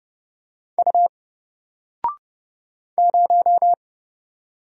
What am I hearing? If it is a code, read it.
Morse code, heard as a keyed tone, about 31 wpm, 700 Hz. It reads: U 0
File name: Single-Letter-Number-30wpm